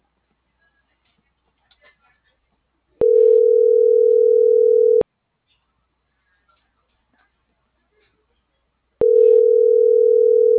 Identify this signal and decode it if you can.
Ringback tone (North America), 440 + 480 Hz.